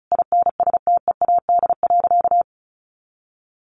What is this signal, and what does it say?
Morse code, heard as a keyed tone, about 34 wpm, 700 Hz. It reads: INSTEAD.
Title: Sentences from Top 1500 Words 35wpm